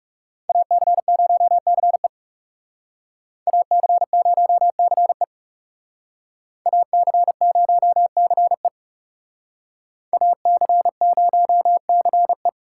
Morse code, heard as a keyed tone, about 34 wpm, 690 Hz. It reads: AC0CE AC0CE AC0CE AC0CE
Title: Speed Racing - USA Call Signs 30wpm